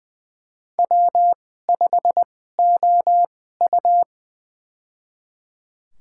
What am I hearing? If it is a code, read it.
Morse code, heard as a keyed tone, about 21 wpm, spaced as for 15 wpm, 700 Hz. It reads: W5OU
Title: USA Call Signs - 2x Character Spacing - 20wpm